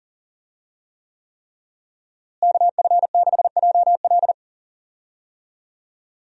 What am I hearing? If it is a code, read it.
Morse code, heard as a keyed tone, about 40 wpm, 700 Hz. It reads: KF6JL
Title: USA Call Signs - 5 Characters - Repeated 3x - 40wpm